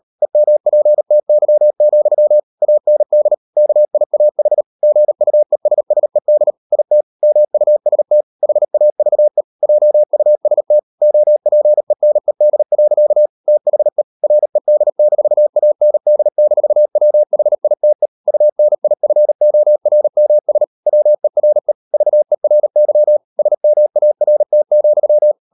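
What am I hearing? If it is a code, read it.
Morse code, heard as a keyed tone, about 38 wpm, 600 Hz. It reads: EMPTY, AND KIAH GUESSED IT MUST HAVE JUST OPENED. THE RED-AND-WHITE UNIFORMS WERE VERY SMART,
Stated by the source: Morse code practice at 38 WPM, sending Ken Follett's novel Never in English, from CW med Gnister.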